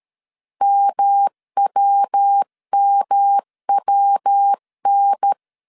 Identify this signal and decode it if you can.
Morse code, heard as a keyed tone, about 13 wpm, 800 Hz. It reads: MWMWN
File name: CW / DET / 1 3 1500C S A 12wpm 2905